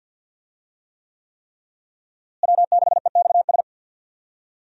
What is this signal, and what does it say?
Morse code, heard as a keyed tone, about 52 wpm, 700 Hz. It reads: W6EXS
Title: USA Call Signs - 5 Characters - Repeated 3x - 50wpm